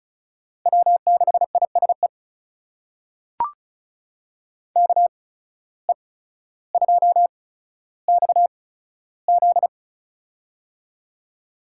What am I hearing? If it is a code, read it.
Morse code, heard as a keyed tone, about 34 wpm, spaced as for 15 wpm, 700 Hz. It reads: W6ISE KE2XZ
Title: USA Call Signs - 5 Characters - 8x Character Spacing - 35wpm